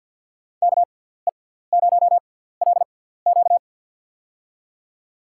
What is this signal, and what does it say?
Morse code, heard as a keyed tone, about 53 wpm, spaced as for 17 wpm, 700 Hz. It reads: KE0LQ
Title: USA Call Signs - 6x Character Spacing - 50wpm